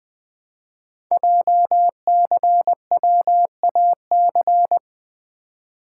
Morse code, heard as a keyed tone, about 20 wpm, 700 Hz. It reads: JCWAC
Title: Code Groups - 5 Letters - 20wpm